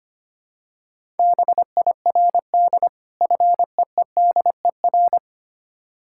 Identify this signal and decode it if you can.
Morse code, heard as a keyed tone, about 25 wpm, 690 Hz. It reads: BIRD FEEDER
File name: practice / 25wpm / Difficult Words with Hints - 25wpm